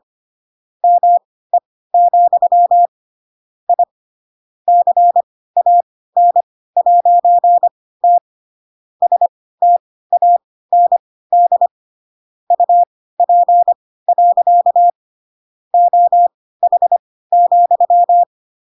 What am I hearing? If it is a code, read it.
Morse code, heard as a keyed tone, about 25 wpm, spaced as for 16 wpm, 700 Hz. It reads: ME, I CAN'T STAND UP. OH,
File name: Pinocchio / Pinocchio ch7 10 25 ik1hge